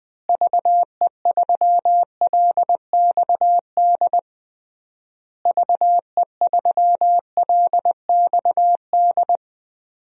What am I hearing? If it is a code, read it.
Morse code, heard as a keyed tone, about 20 wpm, 700 Hz. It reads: VE3LXD VE3LXD
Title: DXCC-100 Call Signs - 6 Characters - Repeated 3x - 20wpm